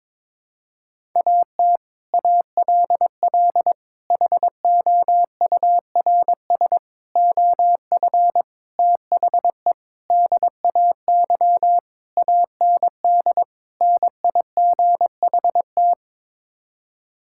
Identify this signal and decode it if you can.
Morse code, heard as a keyed tone, about 22 wpm, 700 Hz. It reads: AT ALL HOURS OF THE DAY AND NIGHT
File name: practice / 22wpm / Sets of 8 Words - 22wpm